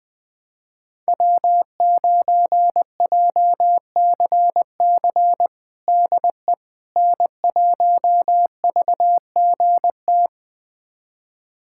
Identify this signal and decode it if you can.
Morse code, heard as a keyed tone, about 20 wpm, 700 Hz. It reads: W9JCC DE N1VGT
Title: USA Callsign DE Callsign - 5 Characters - 20wpm